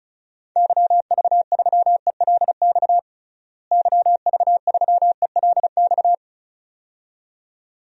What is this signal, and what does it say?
Morse code, heard as a keyed tone, about 36 wpm, 700 Hz. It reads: YV3ELX YV3ELX
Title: DXCC-100 Call Signs - 6 Characters - Repeated 2x - 35wpm